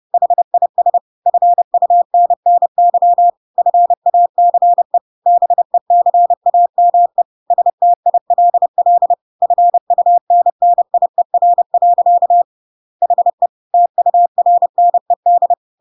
Morse code, heard as a keyed tone, about 30 wpm, 700 Hz. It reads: HIS FUNNY FACE BECAME STILL FUNNIER. HE TURNED
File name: Pinocchio / Pinocchio ch1 30 30 ik1hge